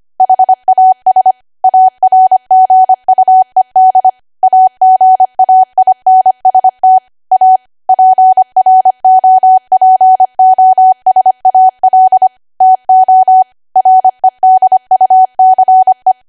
Morse code, heard as a keyed tone, about 25 wpm, 750 Hz. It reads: HAS ARGUED AGAINST A PROPOSAL TO REDUCE